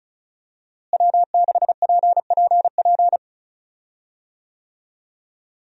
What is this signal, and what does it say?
Morse code, heard as a keyed tone, about 36 wpm, 700 Hz. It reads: W6PPP
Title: USA Call Signs - 5 Characters - Repeated 2x - 35wpm